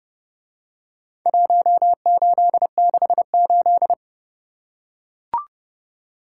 Morse code, heard as a keyed tone, about 28 wpm, 700 Hz. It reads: 1868